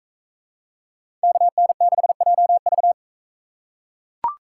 Morse code, heard as a keyed tone, about 42 wpm, 700 Hz. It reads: KN6JV